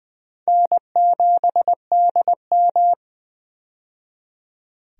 Morse code, heard as a keyed tone, about 20 wpm, 700 Hz. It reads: N7DM
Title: USA Call Signs - 4 Characters - 20wpm